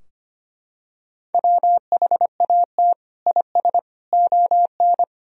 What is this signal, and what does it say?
Morse code, heard as a keyed tone, about 25 wpm, 700 Hz. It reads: WHAT IS ON
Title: Sentences from Top 1000 Words - Review Part 4 - 25wpm